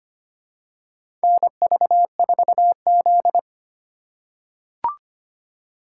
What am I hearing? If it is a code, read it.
Morse code, heard as a keyed tone, about 25 wpm, 700 Hz. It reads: NV4Z
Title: USA Call Signs - 6x Character Spacing - 25wpm